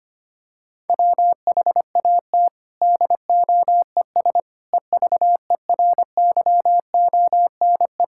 Morse code, heard as a keyed tone, about 25 wpm, 700 Hz. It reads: WHAT DOES EVERYONE
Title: Sentences from Top 1000 Words - Review Part 5 - 25wpm